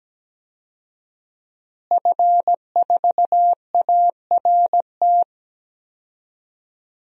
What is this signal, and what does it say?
Morse code, heard as a keyed tone, about 17 wpm, 700 Hz. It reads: F4ART
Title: DXCC-100 Call Signs - 5 Characters - Repeated 2x - 17wpm